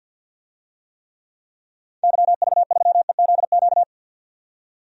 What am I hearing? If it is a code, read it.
Morse code, heard as a keyed tone, about 49 wpm, 700 Hz. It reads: YV3EZQ
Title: DXCC-100 Call Signs - 6 Characters - 50wpm